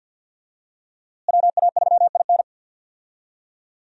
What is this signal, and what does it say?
Morse code, heard as a keyed tone, about 49 wpm, 700 Hz. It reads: WA3IN
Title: USA Call Signs - 5 Characters - Repeated 2x - 50wpm